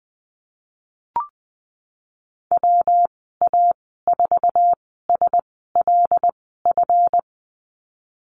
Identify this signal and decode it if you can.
Morse code, heard as a keyed tone, about 20 wpm, spaced as for 14 wpm, 700 Hz. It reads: WA4SLF
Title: 173 Lesson 28g - Number 9 - Call Signs 20wpm